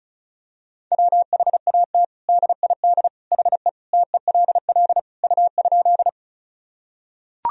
Morse code, heard as a keyed tone, about 35 wpm, 700 Hz. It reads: WHAT DID HE TELL U?
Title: Sentences from Top 200 Words and Abbreviations 35wpm